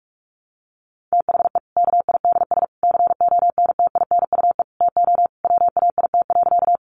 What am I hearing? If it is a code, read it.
Morse code, heard as a keyed tone, about 44 wpm, 700 Hz. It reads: THE KIDS CONTINUE TO WAIT.
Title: Easy Phrases 45wpm